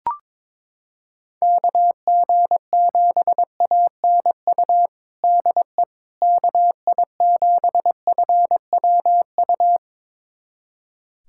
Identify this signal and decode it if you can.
Morse code, heard as a keyed tone, about 21 wpm, 700 Hz. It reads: KG7ANU DE KI7FWU
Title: USA Callsign DE Callsign - 6 Characters - 22wpm